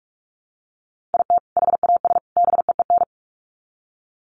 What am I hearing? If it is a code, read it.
Morse code, heard as a keyed tone, about 44 wpm, 700 Hz. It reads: IT HAS BEEN
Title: Sets of 3 Words 45wpm